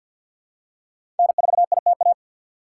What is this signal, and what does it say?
Morse code, heard as a keyed tone, about 48 wpm, 700 Hz. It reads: N4ITA
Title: USA Call Signs - 5 Characters - 2x Character Spacing - 50wpm